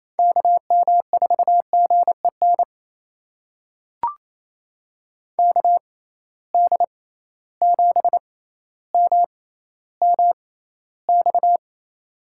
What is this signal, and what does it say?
Morse code, heard as a keyed tone, about 28 wpm, spaced as for 14 wpm, 700 Hz. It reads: KM4GEN KD7MMX
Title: USA Call Signs - 6 Characters - 6x Character Spacing - 28wpm